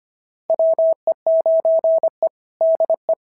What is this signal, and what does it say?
Morse code, heard as a keyed tone, about 22 wpm, 650 Hz. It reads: WE9E DE
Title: SOTA QSO Practice 25wpm